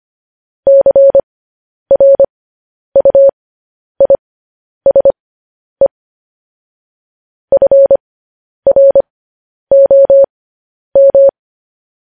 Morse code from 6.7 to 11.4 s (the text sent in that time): FROM